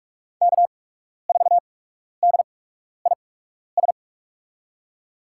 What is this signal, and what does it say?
Morse code, heard as a keyed tone, about 42 wpm, spaced as for 12 wpm, 700 Hz. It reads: K4DIS
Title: USA Call Signs - 5 Characters - 8x Character Spacing - 45wpm